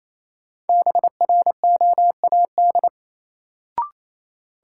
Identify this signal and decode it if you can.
Morse code, heard as a keyed tone, about 28 wpm, 700 Hz. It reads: BROAD
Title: Sentences from Top 1000 Words - New Word Drill - 28wpm